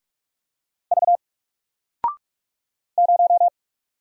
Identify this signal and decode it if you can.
Morse code, heard as a keyed tone, about 46 wpm, 700 Hz. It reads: V 0
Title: Single Character (Letter:Number:Punctuation:Prosign) 45wpm